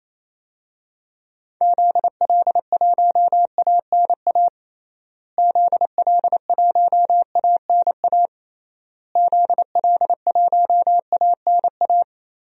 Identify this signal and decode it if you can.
Morse code, heard as a keyed tone, about 28 wpm, 700 Hz. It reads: ZL1ANA ZL1ANA ZL1ANA
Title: DXCC-100 Call Signs - 6 Characters - Repeated 3x - 28wpm